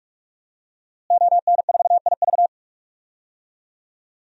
Morse code, heard as a keyed tone, about 46 wpm, 700 Hz. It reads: ON4IV
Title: DXCC-100 Call Signs - 5 Characters - Repeated 3x - 45wpm